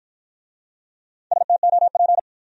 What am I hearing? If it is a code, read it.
Morse code, heard as a keyed tone, about 53 wpm, 700 Hz. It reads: STOP